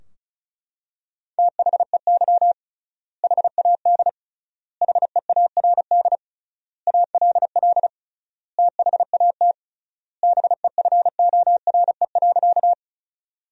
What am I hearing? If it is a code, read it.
Morse code, heard as a keyed tone, about 36 wpm, 700 Hz. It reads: THEY HAD HEARD ALL THAT BEFORE.
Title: 3x Word-Spacing - Sentences from Top 400 Words - 35wpm